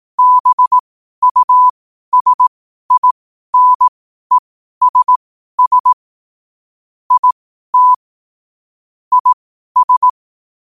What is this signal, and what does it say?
Morse code, heard as a keyed tone, about 18 wpm, spaced as for 12 wpm, 1000 Hz. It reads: BUSINESS IT IS
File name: qotdfiles / 12wpm